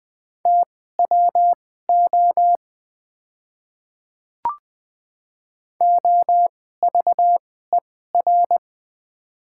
Morse code, heard as a keyed tone, about 20 wpm, spaced as for 14 wpm, 700 Hz. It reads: TWO OVER
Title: 2x Character Spacing - Top 200 Words - 20wpm